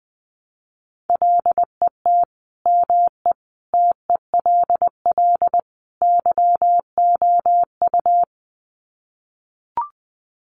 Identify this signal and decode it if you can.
Morse code, heard as a keyed tone, about 20 wpm, 700 Hz. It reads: LET ME TELL YOU
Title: Sets of 4 Words 20wpm